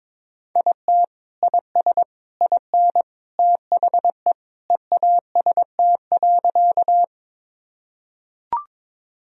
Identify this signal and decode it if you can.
Morse code, heard as a keyed tone, about 22 wpm, 700 Hz. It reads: IT IS IN THE EAST.
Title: Sentences from Top 1000 Words - Review Part 3 - 22wpm